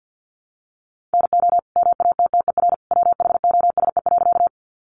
Morse code, heard as a keyed tone, about 50 wpm, 700 Hz. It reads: NO MATTER WHOSE.
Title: Sentences from Top 1200 Words 50wpm